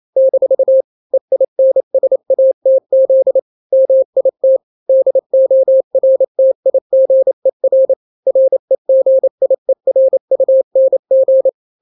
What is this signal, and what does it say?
Morse code, heard as a keyed tone, about 28 wpm, 530 Hz. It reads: - EINSATZ MIT DORTIGER REGIERUNG
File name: daily / 27wpm